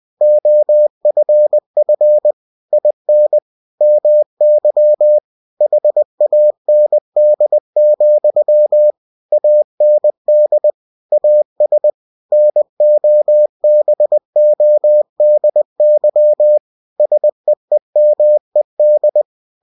Morse code, heard as a keyed tone, about 20 wpm, 600 Hz. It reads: OFF IN MY HAND, AND AS NOBODY SEEMED
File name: cw-books / winnie-the-pooh / winnie-20wpm-page-0032